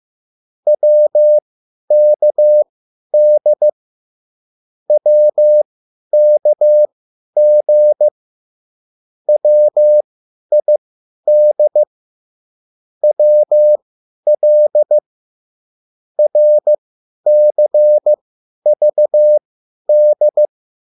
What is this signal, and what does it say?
Morse code, heard as a keyed tone, about 15 wpm, spaced as for 10 wpm, 600 Hz. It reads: WKD WKG WID WL RCVD
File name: QSL / CW lesson